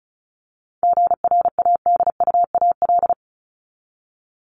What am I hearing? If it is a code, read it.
Morse code, heard as a keyed tone, about 34 wpm, 700 Hz. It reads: GRADUAL
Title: LY Suffix - Top 100 Words with Base Words 35wpm